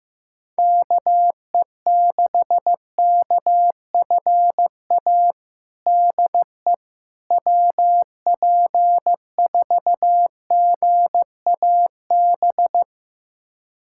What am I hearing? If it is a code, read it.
Morse code, heard as a keyed tone, about 15 wpm, 700 Hz. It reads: KE6KFA DE WP4GAB
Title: USA Callsign DE Callsign - 6 Characters - 15wpm